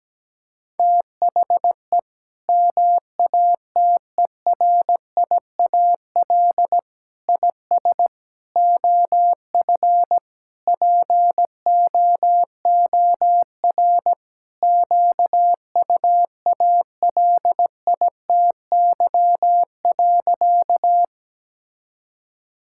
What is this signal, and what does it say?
Morse code, heard as a keyed tone, about 17 wpm, 700 Hz. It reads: THE MATERIAL IS OF POOR QUALITY.
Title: Sentences from Top 1000 Words - Review Part 5 - 17wpm